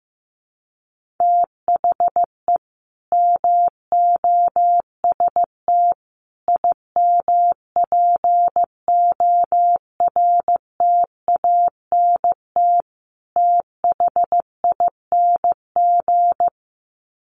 Morse code, heard as a keyed tone, about 15 wpm, 700 Hz. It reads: THE MOST IMPORTANT THING
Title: Sets of 4 Words 15wpm